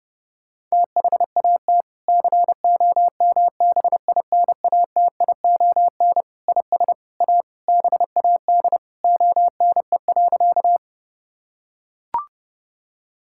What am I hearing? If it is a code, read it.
Morse code, heard as a keyed tone, about 32 wpm, 700 Hz. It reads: THAT COMBINATION IS A BAD ONE.
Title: Sentences from Top 1000 Words - Review Part 6 - 30wpm